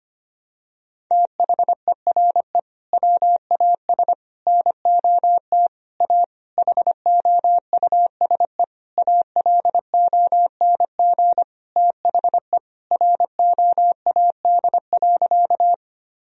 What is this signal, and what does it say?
Morse code, heard as a keyed tone, about 25 wpm, 700 Hz. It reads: THERE WAS NOT A HOUSE ALONG THE ROAD.
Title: Sentences from Top 1000 Words - Review Part 2 - 25wpm